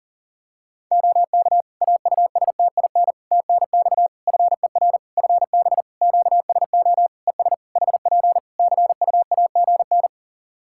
Morse code, heard as a keyed tone, about 40 wpm, 700 Hz. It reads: OK AUSTIN TNX FER FB QSO ES HP CUAGN
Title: Practice QSO - Exchange 4 - Ending - 40wpm